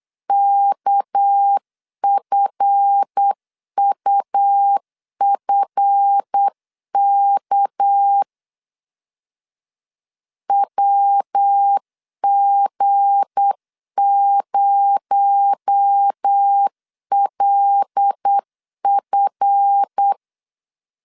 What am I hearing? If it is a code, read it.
Morse code, heard as a keyed tone, about 8 wpm, 800 Hz. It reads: KFUFK WG0LF
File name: LEZIONI / 7 GF9Z / 7 GF 8wpm 3108